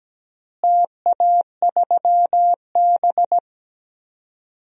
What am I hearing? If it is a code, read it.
Morse code, heard as a keyed tone, about 17 wpm, 700 Hz. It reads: TA3B
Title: DXCC-100 Call Signs - 4 Characters - Repeated 3x - 17wpm